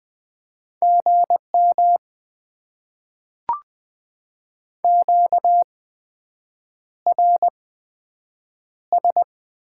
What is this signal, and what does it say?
Morse code, heard as a keyed tone, about 20 wpm, spaced as for 7 wpm, 700 Hz. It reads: GM QRS
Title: Top 90 QSO Elements - 8x Character Spacing - 20wpm